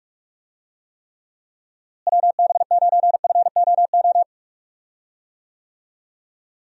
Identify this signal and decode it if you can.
Morse code, heard as a keyed tone, about 46 wpm, 700 Hz. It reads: WB9FOO
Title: USA Call Signs - 6 Characters - Repeated 3x - 45wpm